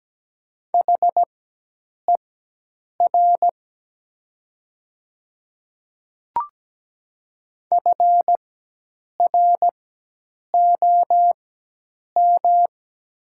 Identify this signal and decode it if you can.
Morse code, heard as a keyed tone, about 17 wpm, spaced as for 8 wpm, 700 Hz. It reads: HER FROM
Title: 4x Character Spacing - Top 100 Words - 17wpm